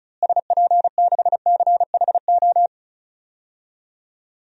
Morse code, heard as a keyed tone, about 34 wpm, 700 Hz. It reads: SP6CHO
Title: DXCC-100 Call Signs - 6 Characters - Repeated 3x - 35wpm